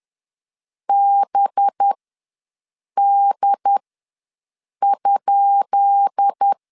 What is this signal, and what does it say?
Morse code, heard as a keyed tone, about 10 wpm, spaced as for 6 wpm, 800 Hz. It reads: BD?
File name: LEZIONI / 10 SIGNS / 10D COMA ws 10wpm 4314